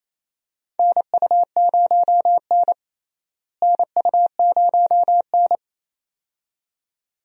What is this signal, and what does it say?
Morse code, heard as a keyed tone, about 28 wpm, 700 Hz. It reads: NU0N NU0N
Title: USA Call Signs - 4 Characters - Repeated 2x - 28wpm